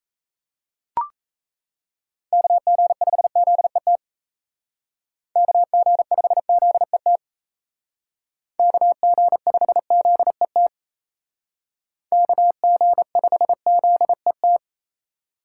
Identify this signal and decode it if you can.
Morse code, heard as a keyed tone, about 37 wpm, 700 Hz. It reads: KG5ZET KG5ZET KG5ZET KG5ZET